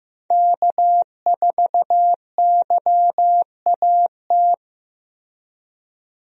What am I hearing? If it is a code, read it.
Morse code, heard as a keyed tone, about 15 wpm, 700 Hz. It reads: K4YAT
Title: USA Call Signs - 5 Characters - Repeated 2x - 15wpm